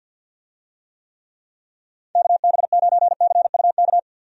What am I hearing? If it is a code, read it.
Morse code, heard as a keyed tone, about 46 wpm, 700 Hz. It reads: KB9CUK
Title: USA Call Signs - 6 Characters - 50wpm